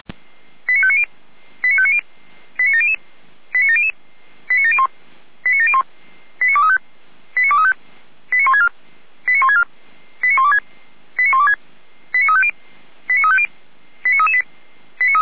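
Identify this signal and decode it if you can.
Five-tone selective call (ZVEI), heard as quick runs of five tones, tones of about 70 ms. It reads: 89590 89590 89700 89700 89701 89701 89235 89235 89164 89164 89126 89126 89350 89350 89338
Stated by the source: Five-tone ZVEI selective call received on an SDR.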